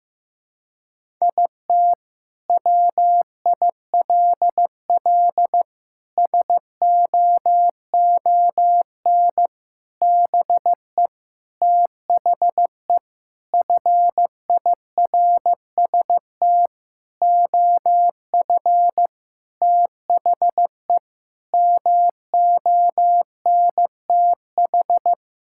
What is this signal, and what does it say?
Morse code, heard as a keyed tone, about 15 wpm, 700 Hz. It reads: IT WILL SOON BE THE FIRST OF THE MONTH
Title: Sentences from Top 1000 Words - Review Part 5 - 15wpm